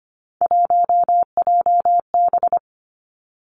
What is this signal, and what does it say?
Morse code, heard as a keyed tone, about 25 wpm, 700 Hz. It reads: 1JB
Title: Three Challenging Letter-Number Combinations 25wpm